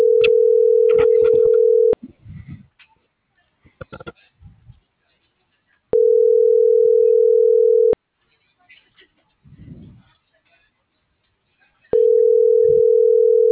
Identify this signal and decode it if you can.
Ringback tone (North America), 440 + 480 Hz.